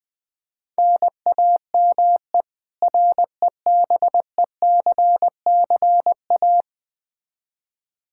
Morse code, heard as a keyed tone, about 20 wpm, 700 Hz. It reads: NAME REBECCA
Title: K1USN SST QSO Practice 20wpm